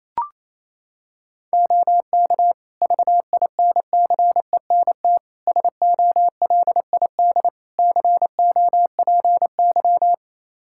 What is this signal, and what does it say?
Morse code, heard as a keyed tone, about 27 wpm, 700 Hz. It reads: OK VINCENT SOLID COPY
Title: Practice QSO - Exchange 3 - More Info - 28wpm